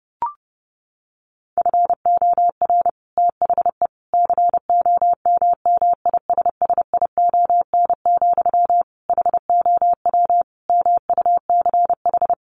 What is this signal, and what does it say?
Morse code, heard as a keyed tone, about 30 wpm, 700 Hz. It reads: FOR THE COMMISSION, HOW MUCH
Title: Sentences from Top 1400 Words 30wpm